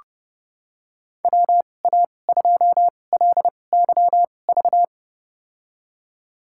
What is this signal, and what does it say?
Morse code, heard as a keyed tone, about 30 wpm, spaced as for 22 wpm, 700 Hz. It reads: WA2LYV
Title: USA Call Signs - 6 Characters - 2x Character Spacing - 30wpm